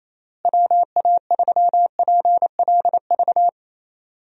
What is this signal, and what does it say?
Morse code, heard as a keyed tone, about 28 wpm, 700 Hz. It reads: WA3PLV
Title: USA Call Signs - 6 Characters - Repeated 2x - 28wpm